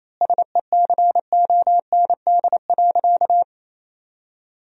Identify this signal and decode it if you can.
Morse code, heard as a keyed tone, about 28 wpm, 700 Hz. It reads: SECOND.
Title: 3x Word-Spacing - Sentences from Top 500 Words - 28wpm